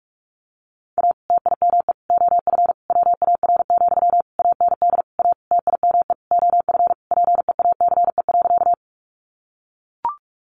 Morse code, heard as a keyed tone, about 44 wpm, 700 Hz. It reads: A TIME OF WAR, AND A TIME OF PEACE.
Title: Sentences from Top 1500 Words 45wpm